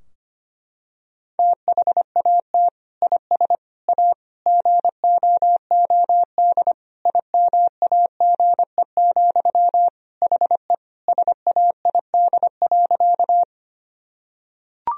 Morse code, heard as a keyed tone, about 25 wpm, 700 Hz. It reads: THAT IS A GOOD IMAGE, HE SAID.
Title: Sentences from Top 1000 Words - Review Part 4 - 25wpm